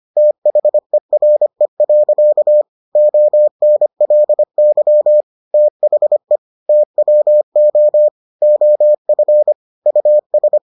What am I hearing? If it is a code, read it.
Morse code, heard as a keyed tone, about 25 wpm, 600 Hz. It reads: THERE. ONLY THE TWO OF US